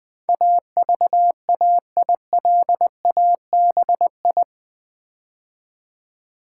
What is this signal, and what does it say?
Morse code, heard as a keyed tone, about 20 wpm, 700 Hz. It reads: AVAILABI
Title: Build A Word - 4 Letters At A Time - 12-Letter Words - 20wpm